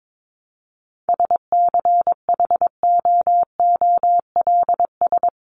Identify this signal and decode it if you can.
Morse code, heard as a keyed tone, about 22 wpm, 700 Hz. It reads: SCHOOLS
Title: ES:S Suffix - Top 100 Words with Base Words 22wpm